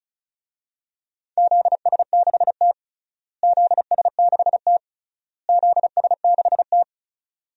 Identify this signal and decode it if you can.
Morse code, heard as a keyed tone, about 34 wpm, 700 Hz. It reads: ZS6T ZS6T ZS6T
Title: DXCC-100 Call Signs - 4 Characters - Repeated 3x - 35wpm